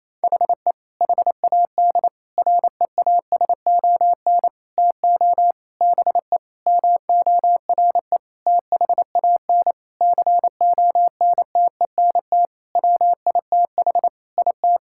Morse code, heard as a keyed tone, about 27 wpm, 700 Hz. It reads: HE HAD REASON TO BE MORE THAN CONTENT WITH IT
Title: Sentences from Top 1000 Words - Review Part 6 - 28wpm